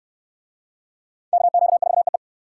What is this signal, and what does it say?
Morse code, heard as a keyed tone, about 67 wpm, 700 Hz. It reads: KQ4EE